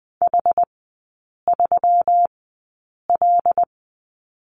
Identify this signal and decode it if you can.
Morse code, heard as a keyed tone, about 20 wpm, spaced as for 8 wpm, 690 Hz. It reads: H3L
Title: 145 Lesson 24e - Letter y - Three Characters 20wpm